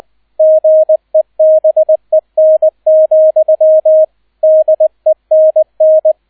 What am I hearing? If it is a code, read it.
Morse code, heard as a keyed tone, about 20 wpm, 630 Hz, with radio noise. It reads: GEBEN, DENN